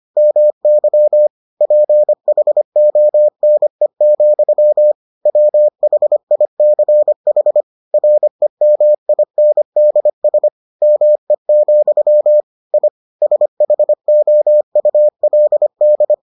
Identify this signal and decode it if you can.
Morse code, heard as a keyed tone, about 25 wpm, 600 Hz. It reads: MY PHONE, WHICH REMINDS ME, I SHOULD